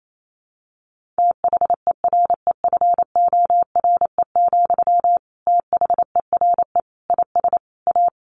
Morse code, heard as a keyed tone, about 28 wpm, 700 Hz. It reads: THEREFORE, THERE IS A
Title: Sentences from Top 1500 Words 28wpm